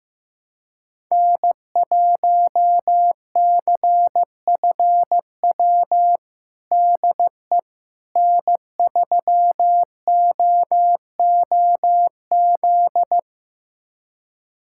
Morse code, heard as a keyed tone, about 16 wpm, 700 Hz. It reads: N1CFW DE N3OOZ
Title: USA Callsign DE Callsign 15wpm